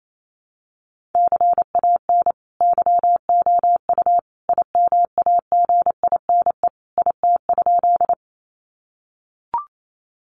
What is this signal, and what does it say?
Morse code, heard as a keyed tone, about 28 wpm, 700 Hz. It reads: CAN YOU IMAGINE IT?
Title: Sentences from Top 1400 Words 28wpm